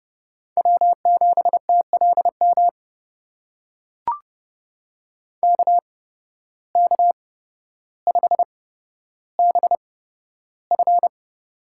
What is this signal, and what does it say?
Morse code, heard as a keyed tone, about 30 wpm, spaced as for 13 wpm, 700 Hz. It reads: W7TLM KK5BF
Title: USA Call Signs - 5 Characters - 8x Character Spacing - 30wpm